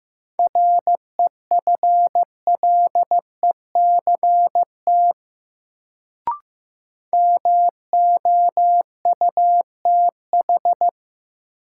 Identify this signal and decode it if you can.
Morse code, heard as a keyed tone, about 15 wpm, 700 Hz. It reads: REFLECT MOUTH
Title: Sentences from Top 1500 Words - New Word Drill - 15wpm